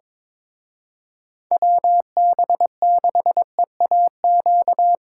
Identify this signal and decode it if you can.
Morse code, heard as a keyed tone, about 25 wpm, 700 Hz. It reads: WB6EAQ
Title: Speed Racing - USA 2x3 Call Signs 22wpm